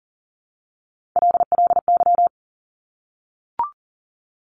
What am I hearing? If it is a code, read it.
Morse code, heard as a keyed tone, about 42 wpm, 700 Hz. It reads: LLY
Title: Top Three Letter Combinations in Words 40wpm